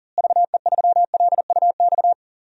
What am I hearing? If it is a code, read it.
Morse code, heard as a keyed tone, about 40 wpm, 700 Hz. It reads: VE3LUX
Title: DXCC-100 Call Signs - 6 Characters - Repeated 3x - 40wpm